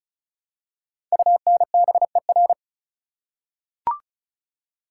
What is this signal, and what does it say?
Morse code, heard as a keyed tone, about 36 wpm, 690 Hz. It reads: UNBER
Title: Code Groups - 5 Letters - 35wpm